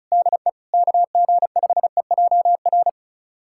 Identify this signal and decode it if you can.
Morse code, heard as a keyed tone, about 36 wpm, 700 Hz. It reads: DE KG5EJR
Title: USA Callsign DE Callsign - 6 Characters - 35wpm